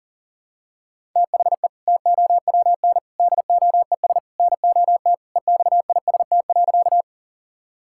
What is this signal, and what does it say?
Morse code, heard as a keyed tone, about 40 wpm, 700 Hz. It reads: THE TOWN DOES NOT EXIST.
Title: Sentences from Top 1000 Words - Review Part 5 - 40wpm